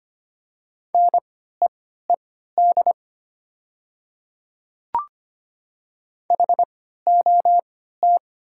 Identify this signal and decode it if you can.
Morse code, heard as a keyed tone, about 25 wpm, spaced as for 14 wpm, 700 Hz. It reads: NEED HOT
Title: 3x Character Spacing - Top 200 Words - 25wpm